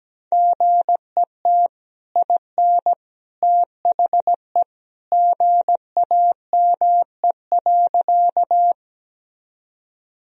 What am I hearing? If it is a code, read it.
Morse code, heard as a keyed tone, about 17 wpm, 700 Hz. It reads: GET IN THE GAME.
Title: Sentences from Top 1000 Words - Review Part 3 - 17wpm